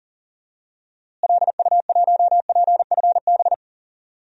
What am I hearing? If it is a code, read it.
Morse code, heard as a keyed tone, about 40 wpm, 700 Hz. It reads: LU1PFB
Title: DXCC-100 Call Signs - 6 Characters - Repeated 2x - 40wpm